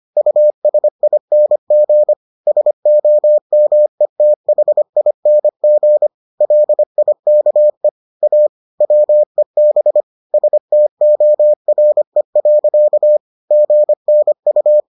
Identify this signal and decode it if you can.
Morse code, heard as a keyed tone, about 25 wpm, 600 Hz. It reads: USING SOMETHING LIKE A WEB STORE. GNU